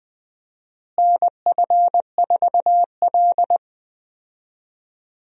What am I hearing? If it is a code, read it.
Morse code, heard as a keyed tone, about 20 wpm, 700 Hz. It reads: NF4L
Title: USA Call Signs - 4 Characters - Repeated 3x - 20wpm